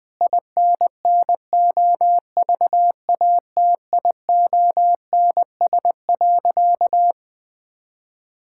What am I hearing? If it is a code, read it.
Morse code, heard as a keyed tone, about 21 wpm, 700 Hz. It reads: INNOVATIONS.
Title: Sentences with 8-Word Phrases - 20wpm